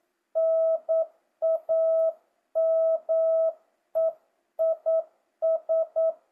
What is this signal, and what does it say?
Morse code, heard as a keyed tone, about 9 wpm, 650 Hz. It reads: NAME IS
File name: jh8cbh / 2025cw